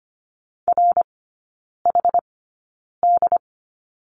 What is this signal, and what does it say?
Morse code, heard as a keyed tone, about 25 wpm, spaced as for 9 wpm, 690 Hz. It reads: RHD
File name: Lesson 14e - Number 2 - Three Characters 25wpm